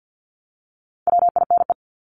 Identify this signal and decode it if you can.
Morse code, heard as a keyed tone, about 50 wpm, 700 Hz. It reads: WINE
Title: Top 1501-1750 Words 50wpm